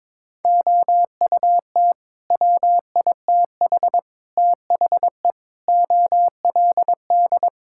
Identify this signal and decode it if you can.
Morse code, heard as a keyed tone, about 22 wpm, 700 Hz. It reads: OUT WITH THE OLD
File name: Sets of 9 Words - 22wpm